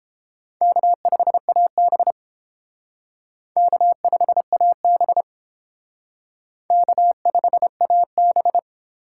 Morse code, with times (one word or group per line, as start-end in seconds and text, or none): 0.6-2.1 K5AB
3.6-5.2 K5AB
6.7-8.6 K5AB